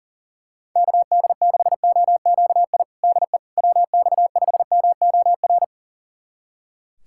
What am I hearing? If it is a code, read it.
Morse code, heard as a keyed tone, about 40 wpm, 700 Hz. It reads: KD6OQI DE WX5MOR
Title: USA Callsign DE Callsign - 6 Characters - 40wpm